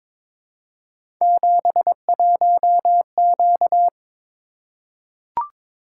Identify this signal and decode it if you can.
Morse code, heard as a keyed tone, about 22 wpm, 700 Hz. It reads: 71Q
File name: practice / 22wpm / Three Letter-Number Combinations - 22wpm